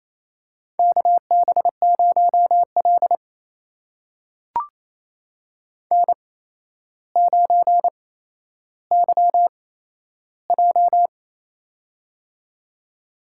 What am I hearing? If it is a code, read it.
Morse code, heard as a keyed tone, about 27 wpm, spaced as for 12 wpm, 700 Hz. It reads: KB0L N9YJ
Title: USA Call Signs - 4 Characters - 8x Character Spacing - 28wpm